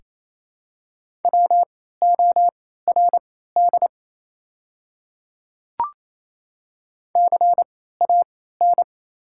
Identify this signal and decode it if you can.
Morse code, heard as a keyed tone, about 28 wpm, spaced as for 16 wpm, 690 Hz. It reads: WORD CAN
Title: 3x Character Spacing - Top 200 Words - 28wpm